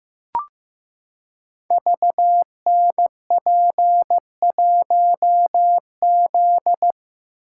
Morse code, heard as a keyed tone, about 15 wpm, 700 Hz. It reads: VNP1Z
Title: Code Groups - 5 Letter:Numbers - 15wpm